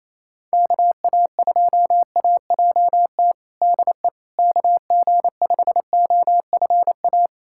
Morse code, heard as a keyed tone, about 28 wpm, 700 Hz. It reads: KA2AJT DE KG5OFA
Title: USA Callsign DE Callsign - 6 Characters - 28wpm